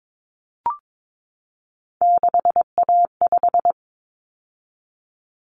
Morse code, heard as a keyed tone, about 22 wpm, 700 Hz. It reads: 6A5
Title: Common Cut Numbers - 3-Digit 22wpm